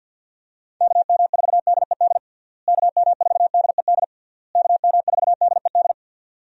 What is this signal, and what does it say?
Morse code, heard as a keyed tone, about 50 wpm, 700 Hz. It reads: KM4DED KM4DED KM4DED
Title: USA Call Signs - 6 Characters - Repeated 3x - 50wpm